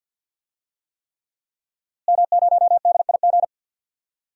Morse code, heard as a keyed tone, about 52 wpm, 700 Hz. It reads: M0DIG
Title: DXCC-100 Call Signs - 5 Characters - Repeated 3x - 50wpm